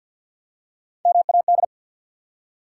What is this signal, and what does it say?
Morse code, heard as a keyed tone, about 53 wpm, 700 Hz. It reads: MAD
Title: Three Letter Words - 50wpm